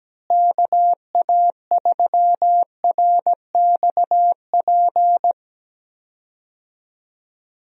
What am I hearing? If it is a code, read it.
Morse code, heard as a keyed tone, about 17 wpm, 700 Hz. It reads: KA3RXP
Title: USA Call Signs - 6 Characters - Repeated 3x - 17wpm